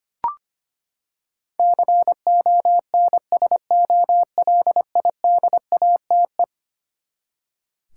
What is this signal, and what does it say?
Morse code, heard as a keyed tone, about 25 wpm, 700 Hz. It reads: CONSOLIDATE
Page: morse-code-ninja-practice-25wpm